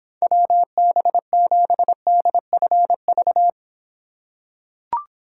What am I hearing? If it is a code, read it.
Morse code, heard as a keyed tone, about 26 wpm, 700 Hz. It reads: WB7DFV